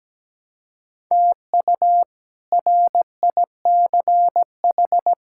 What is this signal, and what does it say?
Morse code, heard as a keyed tone, about 17 wpm, 700 Hz. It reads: TU RICH
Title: K1USN SST QSO Practice - 17wpm